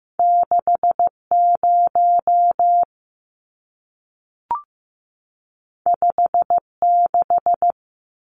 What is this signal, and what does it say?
Morse code, heard as a keyed tone, about 15 wpm, 700 Hz. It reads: 60 56
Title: Numbers - 2-Digit 15wpm